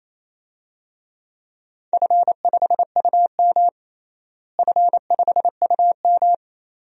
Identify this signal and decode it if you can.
Morse code, heard as a keyed tone, about 28 wpm, 700 Hz. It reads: F5UM F5UM
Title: DXCC-100 Call Signs - 4 Characters - Repeated 2x - 28wpm